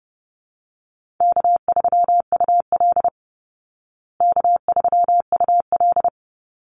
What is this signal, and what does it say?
Morse code, heard as a keyed tone, about 30 wpm, 700 Hz. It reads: K3UL K3UL
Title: Easy Call Signs Repeated 3x - 30wpm